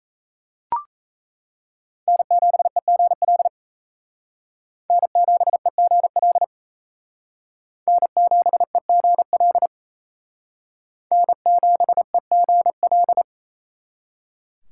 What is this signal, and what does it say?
Morse code, heard as a keyed tone, about 35 wpm, 700 Hz. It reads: N7EGL N7EGL N7EGL N7EGL
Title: Speed Racing - USA 1x3 Call Signs 28wpm